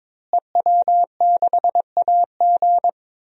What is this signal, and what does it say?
Morse code, heard as a keyed tone, about 23 wpm, 700 Hz. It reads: EW6AG
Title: DXCC-100 Call Signs - 5 Characters - Repeated 3x - 22wpm